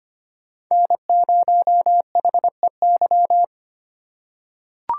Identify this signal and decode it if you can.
Morse code, heard as a keyed tone, about 25 wpm, 700 Hz. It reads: N0HEY
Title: USA Call Signs - 5 Characters - 3x Character Spacing - 25wpm